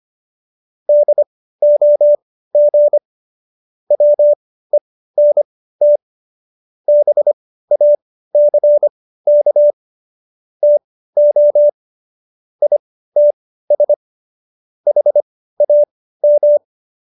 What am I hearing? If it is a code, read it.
Morse code, heard as a keyed tone, about 25 wpm, spaced as for 15 wpm, 590 Hz. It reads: DOG WENT BACK TO ITS HAM